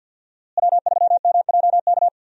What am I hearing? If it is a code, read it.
Morse code, heard as a keyed tone, about 52 wpm, 700 Hz. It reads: W3MJK
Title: USA Call Signs - 5 Characters - Repeated 2x - 50wpm